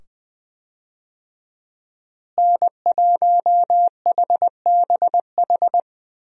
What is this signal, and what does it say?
Morse code, heard as a keyed tone, about 20 wpm, 700 Hz. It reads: N1HBH
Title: USA Call Signs - 5 Characters - Repeated 2x - 20wpm